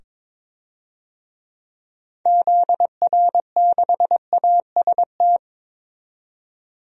Morse code, heard as a keyed tone, about 22 wpm, 700 Hz. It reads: ZR6AST
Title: DXCC-100 Call Signs - 6 Characters - 22wpm